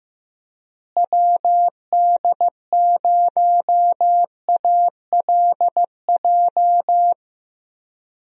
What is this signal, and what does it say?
Morse code, heard as a keyed tone, about 15 wpm, 700 Hz. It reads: WD0ALJ
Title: USA Call Signs - 6 Characters - Repeated 3x - 15wpm